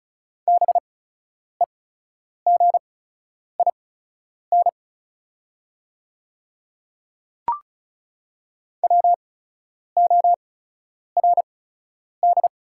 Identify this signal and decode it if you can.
Morse code, heard as a keyed tone, about 34 wpm, spaced as for 10 wpm, 690 Hz. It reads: BEGIN WORD
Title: 8x Character Spacing - Top 500 Words - 35wpm